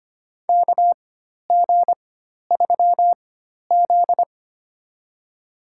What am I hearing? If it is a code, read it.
Morse code, heard as a keyed tone, about 25 wpm, spaced as for 12 wpm, 700 Hz. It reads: KG3Z
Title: USA Call Signs - 4 Characters - 4x Character Spacing - 25wpm